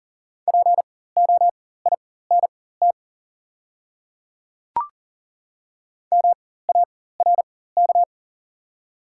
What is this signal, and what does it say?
Morse code, heard as a keyed tone, about 39 wpm, spaced as for 18 wpm, 690 Hz. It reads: POINT MARK
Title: 4x Character Spacing - Top 500 Words - 40wpm